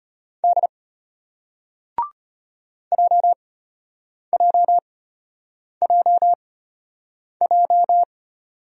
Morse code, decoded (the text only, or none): D JJJJ